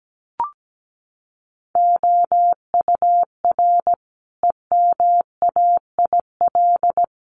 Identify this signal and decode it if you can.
Morse code, heard as a keyed tone, about 17 wpm, 700 Hz. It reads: OUR EMAIL
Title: Sets of 5 Words Encore2 17wpm